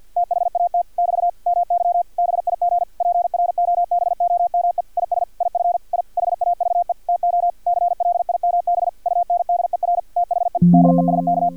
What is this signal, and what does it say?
Morse code, heard as a keyed tone, about 50 wpm, 700 Hz. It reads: THAT <BT> MY BIG PROBOME IS IF I HAVE TO CLIMB UNDER THE DESK
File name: archived / CRSnet